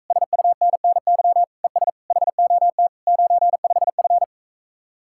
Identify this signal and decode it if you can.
Morse code, heard as a keyed tone, about 45 wpm, 700 Hz. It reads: SUNNY ES HOT 95F